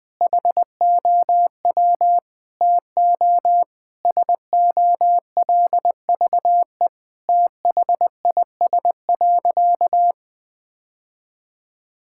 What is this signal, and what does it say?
Morse code, heard as a keyed tone, about 21 wpm, 700 Hz. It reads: HOW TO SOLVE THIS.
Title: Sentences with 5-Word Phrases - 20wpm